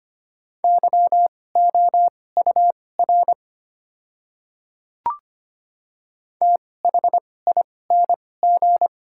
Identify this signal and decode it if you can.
Morse code, heard as a keyed tone, about 25 wpm, spaced as for 18 wpm, 690 Hz. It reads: YOUR THING
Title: 2x Character Spacing - Top 100 Words - 25wpm